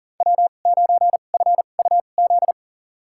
Morse code, heard as a keyed tone, about 40 wpm, spaced as for 29 wpm, 700 Hz. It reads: W9FUZ
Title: USA Call Signs - 5 Characters - 2x Character Spacing - 40wpm